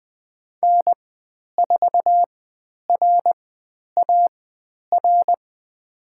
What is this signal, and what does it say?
Morse code, heard as a keyed tone, about 20 wpm, spaced as for 10 wpm, 700 Hz. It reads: N4RAR
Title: 109 Lesson 18g - Letter m - Call Signs 20wpm